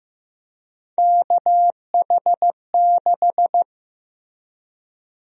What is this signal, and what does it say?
Morse code, heard as a keyed tone, about 15 wpm, 700 Hz. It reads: KH6
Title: USA Call Sign Prefixes - Repeated 2x - 15wpm